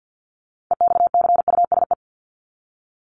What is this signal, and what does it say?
Morse code, heard as a keyed tone, about 50 wpm, 700 Hz. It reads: EXCUSE
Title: Top 501-750 Words 50wpm